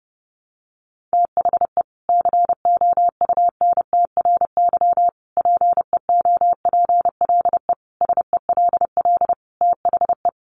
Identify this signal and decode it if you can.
Morse code, heard as a keyed tone, about 30 wpm, 700 Hz. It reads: THE COUNTRY PEOPLE SELL THE
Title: Sentences from Top 1100 Words 30wpm